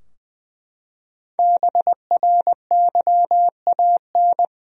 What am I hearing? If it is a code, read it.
Morse code, heard as a keyed tone, about 21 wpm, 700 Hz. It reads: BRYAN
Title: USA Common Names - Max 5 Letters - 20wpm